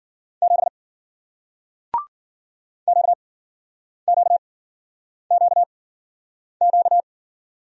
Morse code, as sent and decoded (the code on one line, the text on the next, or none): --... / --.- --.- --.- --.-
7 QQQQ